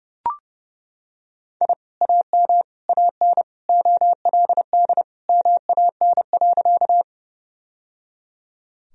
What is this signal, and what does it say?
Morse code, heard as a keyed tone, about 30 wpm, 700 Hz. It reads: I AM AN OLD MAN.
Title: Sentences from Top 1000 Words - Review Part 2 - 30wpm